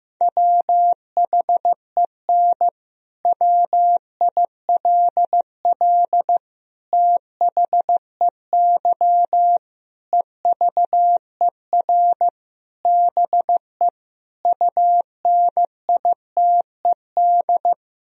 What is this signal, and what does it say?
Morse code, heard as a keyed tone, about 14 wpm, 700 Hz. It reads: WHEN WILL THEY EVER BE UNITED
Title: Sentences from Top 1000 Words - Review Part 4 - 15wpm